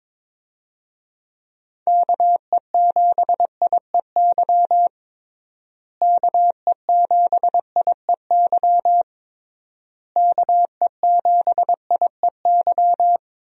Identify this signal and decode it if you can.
Morse code, heard as a keyed tone, about 22 wpm, 700 Hz. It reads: KE7IEY KE7IEY KE7IEY
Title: USA Call Signs - 6 Characters - Repeated 3x - 22wpm